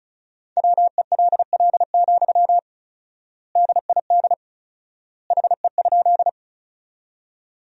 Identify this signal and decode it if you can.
Morse code, heard as a keyed tone, about 36 wpm, 700 Hz. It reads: WELL, DID HE?
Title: 4x Word-Spacing - Sentences from Top 200 Words - 35wpm